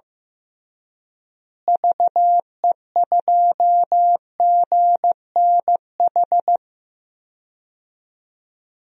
Morse code, heard as a keyed tone, about 15 wpm, 700 Hz. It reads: VE2GNH
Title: DXCC-100 Call Signs - 6 Characters - Repeated 3x - 15wpm